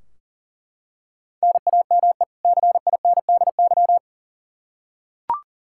Morse code, heard as a keyed tone, about 39 wpm, 700 Hz. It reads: NAME CINDY
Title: K1USN SST QSO Practice 40wpm